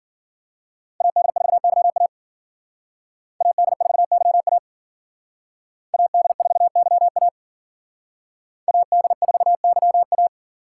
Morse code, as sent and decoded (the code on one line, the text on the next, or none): .- -.. ....- -.-- .- / .- -.. ....- -.-- .- / .- -.. ....- -.-- .- / .- -.. ....- -.-- .-
AD4YA AD4YA AD4YA AD4YA